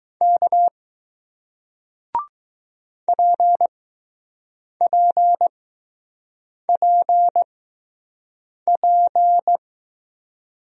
Morse code, heard as a keyed tone, about 20 wpm, spaced as for 6 wpm, 700 Hz. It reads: K PPPP